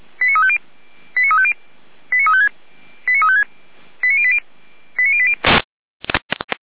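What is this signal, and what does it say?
Five-tone selective call (ZVEI), heard as quick runs of five tones, tones of about 70 ms. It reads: 89350 89350 89356 89356 89080 89080
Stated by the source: ZVEI five-tone selective call received on SDR, archived by Feuerwehr Orlamünde (F85360).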